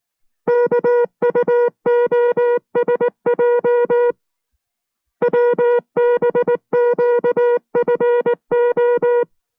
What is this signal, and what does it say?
Morse code, heard as a keyed tone, about 19 wpm, 480 Hz. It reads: KUOSJ WBQFO